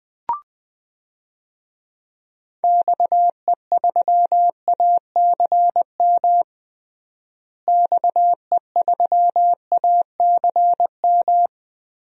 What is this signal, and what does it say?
Morse code, heard as a keyed tone, about 20 wpm, 700 Hz. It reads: XE3ACM XE3ACM
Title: DXCC-100 Call Signs - 6 Characters - Repeated 3x - 20wpm